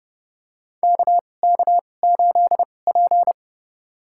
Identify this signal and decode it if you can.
Morse code, heard as a keyed tone, about 30 wpm, spaced as for 22 wpm, 700 Hz. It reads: KK8P
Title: USA Call Signs - 4 Characters - 2x Character Spacing - 30wpm